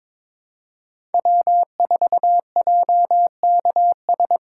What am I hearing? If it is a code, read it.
Morse code, heard as a keyed tone, about 22 wpm, 700 Hz. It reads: W4JKS
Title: USA Call Signs - 5 Characters - 2x Character Spacing - 22wpm